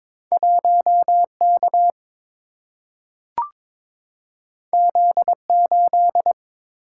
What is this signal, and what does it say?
Morse code, heard as a keyed tone, about 22 wpm, 700 Hz. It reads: 1K Z8
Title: Single-Letter Single-Number 22wpm